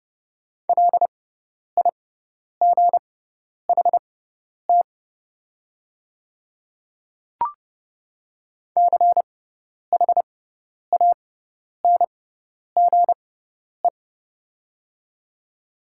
Morse code, heard as a keyed tone, about 30 wpm, spaced as for 10 wpm, 700 Hz. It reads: LIGHT CHANGE
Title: 6x Character Spacing - Top 300 Words - 30wpm